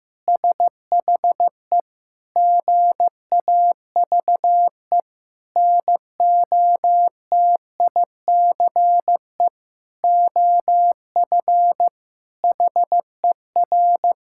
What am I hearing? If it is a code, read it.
Morse code, heard as a keyed tone, about 15 wpm, 700 Hz. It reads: SHE GAVE NOTICE OF HER
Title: Sentences from Top 1000 Words - Review Part 6 - 15wpm